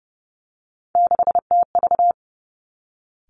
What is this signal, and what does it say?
Morse code, heard as a keyed tone, about 30 wpm, 700 Hz. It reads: 6TV
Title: Three Challenging Letter-Number Combinations 30wpm